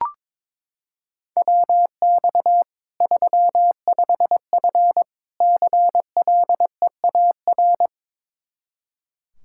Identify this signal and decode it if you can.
Morse code, heard as a keyed tone, about 21 wpm, 700 Hz. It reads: WX 35F CLEAR
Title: K1USN SST QSO Practice 22wpm